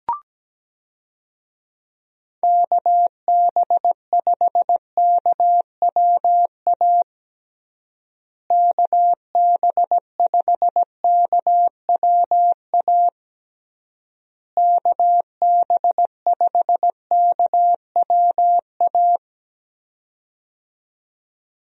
Morse code, heard as a keyed tone, about 17 wpm, 700 Hz. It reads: KB5KWA KB5KWA KB5KWA
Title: USA Call Signs - 6 Characters - Repeated 3x - 17wpm